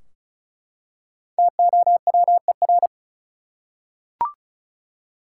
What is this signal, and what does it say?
Morse code, heard as a keyed tone, about 36 wpm, 700 Hz. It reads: TOWER
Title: Five Letter Words - 35wpm